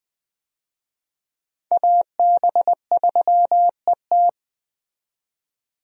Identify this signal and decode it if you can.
Morse code, heard as a keyed tone, about 21 wpm, 700 Hz. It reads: AB3ET
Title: USA Call Signs - 5 Characters - 20wpm